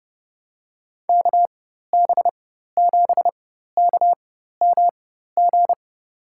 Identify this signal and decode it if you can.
Morse code, heard as a keyed tone, about 30 wpm, spaced as for 14 wpm, 700 Hz. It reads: KB7KMG
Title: USA Call Signs - 6 Characters - 4x Character Spacing - 30wpm